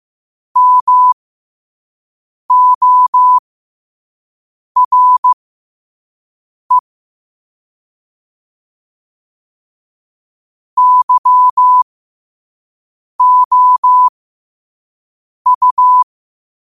Morse code, heard as a keyed tone, about 15 wpm, spaced as for 5 wpm, 1000 Hz. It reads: MORE YOU
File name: qotdfiles / 5wpm